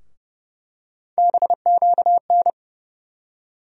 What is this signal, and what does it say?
Morse code, heard as a keyed tone, about 30 wpm, 700 Hz. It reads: BQN
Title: USA Call Sign Postfixes - 3 Letters 30wpm